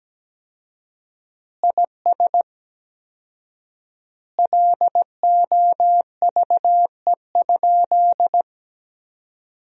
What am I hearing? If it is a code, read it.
Morse code, heard as a keyed tone, about 17 wpm, 700 Hz. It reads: IS LOVE?
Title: 4x Word-Spacing - Sentences from Top 400 Words - 17wpm